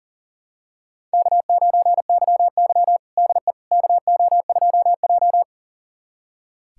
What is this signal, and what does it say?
Morse code, heard as a keyed tone, about 40 wpm, 700 Hz. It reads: K9YY DE KO2J
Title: USA Callsign DE Callsign - 4 Characters - 40wpm